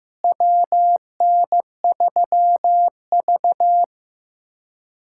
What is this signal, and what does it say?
Morse code, heard as a keyed tone, about 15 wpm, 700 Hz. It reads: WN3V
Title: USA Callsign DE Callsign - 4 Characters - 15wpm